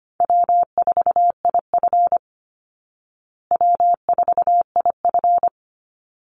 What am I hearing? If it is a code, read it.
Morse code, heard as a keyed tone, about 25 wpm, 700 Hz. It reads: W4IF W4IF
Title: Easy Call Signs Repeated 3x - 25wpm